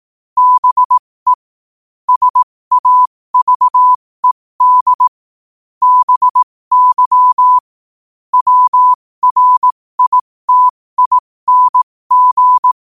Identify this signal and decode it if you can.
Morse code, heard as a keyed tone, about 18 wpm, spaced as for 15 wpm, 1000 Hz. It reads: BE SAVED BY WRITING